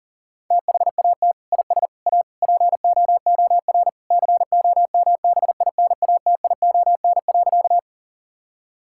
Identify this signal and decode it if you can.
Morse code, heard as a keyed tone, about 40 wpm, 700 Hz. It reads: THAT IS A POOR COMBINATION.
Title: Sentences from Top 1000 Words - Review Part 6 - 40wpm